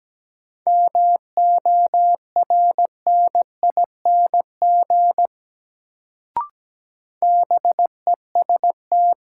Morse code, heard as a keyed tone, about 17 wpm, 700 Hz. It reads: MORNING BEST
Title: Sentences from Top 400 Words - New Word Drill - 17wpm